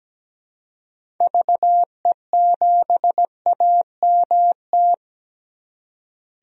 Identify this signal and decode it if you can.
Morse code, heard as a keyed tone, about 17 wpm, 700 Hz. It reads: VE7AMT
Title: DXCC-100 Call Signs - 6 Characters - Repeated 2x - 17wpm